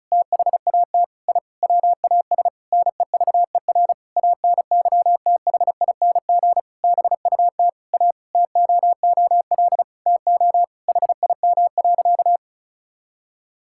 Morse code, heard as a keyed tone, about 36 wpm, 690 Hz. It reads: THAT I WAS NEVER ANYTHING BUT A TOOL TO HIM.
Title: Sentences from Top 1000 Words - Review Part 6 - 35wpm